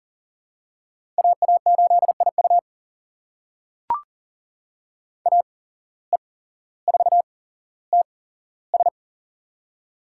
Morse code, heard as a keyed tone, about 40 wpm, spaced as for 17 wpm, 700 Hz. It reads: AA8IU AE4TS
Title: USA Call Signs - 5 Characters - 8x Character Spacing - 40wpm